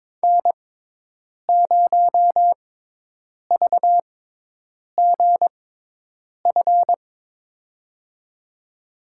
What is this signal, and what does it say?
Morse code, heard as a keyed tone, about 22 wpm, spaced as for 8 wpm, 700 Hz. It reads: N0VGF
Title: USA Call Signs - 6x Character Spacing - 22wpm